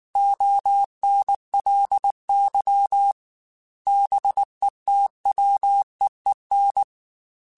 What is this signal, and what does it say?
Morse code, heard as a keyed tone, about 19 wpm, 800 Hz. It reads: ONLY BETWEEN
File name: Weather Sets / Weather Sets-mp3s / Weather 5 1 @18